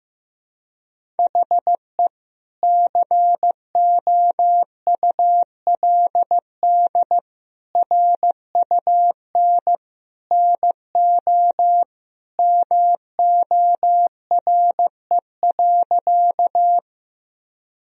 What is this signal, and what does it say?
Morse code, heard as a keyed tone, about 16 wpm, 700 Hz. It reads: HE COULD RUN NO MORE.
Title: Sentences from Top 1000 Words - Review Part 1 - 15wpm